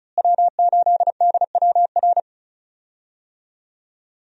Morse code, heard as a keyed tone, about 34 wpm, 700 Hz. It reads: W8DWR
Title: USA Call Signs - 5 Characters - Repeated 3x - 35wpm